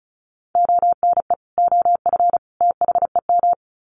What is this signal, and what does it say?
Morse code, heard as a keyed tone, about 34 wpm, 700 Hz. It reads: ONE OF THEM
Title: Sets of 3 Words 35wpm